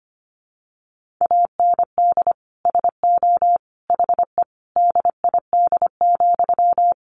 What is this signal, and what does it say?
Morse code, heard as a keyed tone, about 25 wpm, 700 Hz. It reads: AND SO HE DID,
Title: Sentences from Top 1500 Words 25wpm